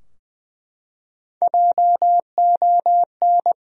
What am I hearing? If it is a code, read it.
Morse code, heard as a keyed tone, about 20 wpm, 700 Hz. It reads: JON
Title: USA Common Names - Max 3 Letters - 20wpm